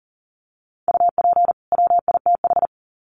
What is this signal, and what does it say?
Morse code, heard as a keyed tone, about 40 wpm, 700 Hz. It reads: UP WITH
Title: Sets of 2 Words 40wpm